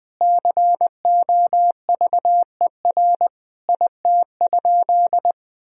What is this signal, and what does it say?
Morse code, heard as a keyed tone, about 21 wpm, 700 Hz. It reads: COVER IT?
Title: Sentences from Top 1000 Words - Review Part 1 - 20wpm